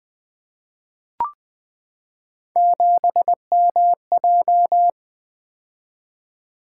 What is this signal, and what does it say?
Morse code, heard as a keyed tone, about 20 wpm, 700 Hz. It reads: 7MJ
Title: Three Letter-Number Combinations - 20wpm